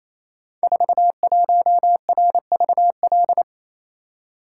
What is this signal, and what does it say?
Morse code, heard as a keyed tone, about 29 wpm, 700 Hz. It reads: 41RVL
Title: Code Groups - 5 Letter:Numbers - 28wpm